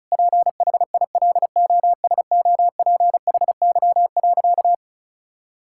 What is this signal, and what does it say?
Morse code, heard as a keyed tone, about 36 wpm, 700 Hz. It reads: PHILOSOPHY.
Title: Sentences from Top 1000 Words - Review Part 6 - 35wpm